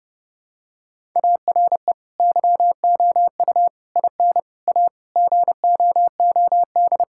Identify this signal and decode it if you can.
Morse code, heard as a keyed tone, about 32 wpm, 700 Hz. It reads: ARE YOU IN A GOOD
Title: Sentences from Top 1000 Words - Review Part 5 - 30wpm